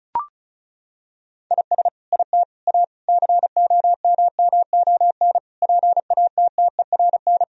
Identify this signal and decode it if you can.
Morse code, heard as a keyed tone, about 34 wpm, 700 Hz. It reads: IS IT A COMMON PATTERN
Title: Sentences from Top 1000 Words - Review Part 2 - 35wpm